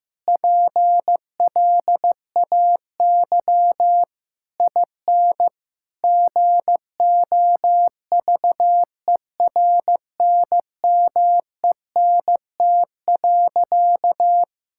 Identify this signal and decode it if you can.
Morse code, heard as a keyed tone, about 16 wpm, 700 Hz. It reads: PLAY IN GOVERNMENT.
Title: Sentences from Top 1000 Words - Review Part 6 - 15wpm